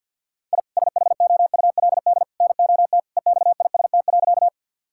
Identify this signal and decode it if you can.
Morse code, heard as a keyed tone, about 50 wpm, 700 Hz. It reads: I SHOULD NOT EXIST.
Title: Sentences from Top 1000 Words - Review Part 5 - 50wpm